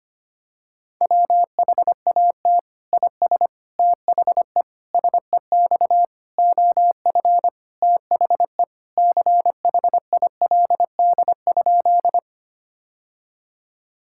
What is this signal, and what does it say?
Morse code, heard as a keyed tone, about 25 wpm, 700 Hz. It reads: WHAT IS THE SEX OF THE CHILD?
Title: Sentences from Top 1600 Words 25wpm